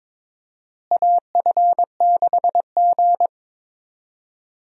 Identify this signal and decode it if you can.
Morse code, heard as a keyed tone, about 21 wpm, 700 Hz. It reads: AF6G
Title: USA Call Signs - 4 Characters - Repeated 3x - 22wpm